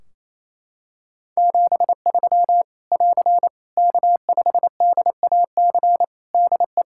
Morse code, heard as a keyed tone, about 28 wpm, 700 Hz. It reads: 73 <AR> K5DAC DE
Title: Practice QSO - Exchange 4 - Ending - 28wpm